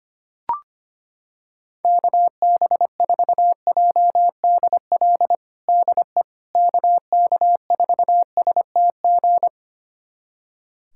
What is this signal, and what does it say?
Morse code, heard as a keyed tone, about 25 wpm, 700 Hz. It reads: KB4JDL DE KK4STG
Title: USA Callsign DE Callsign 25wpm